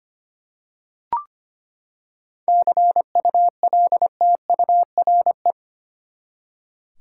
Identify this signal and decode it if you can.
Morse code, heard as a keyed tone, about 25 wpm, 700 Hz. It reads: CULTURE
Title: Seven Letter Words - 25wpm